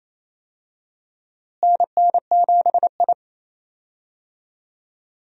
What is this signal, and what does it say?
Morse code, heard as a keyed tone, about 28 wpm, 700 Hz. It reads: NN7I
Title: USA Call Signs - 4 Characters - Repeated 2x - 28wpm